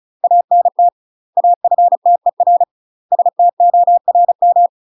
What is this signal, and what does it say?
Morse code, heard as a keyed tone, about 36 wpm, 700 Hz. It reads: ANT AFTER STORM
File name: UK MASTER QSOs TRAINING / 35WPM / UK MASTER QSOs 0028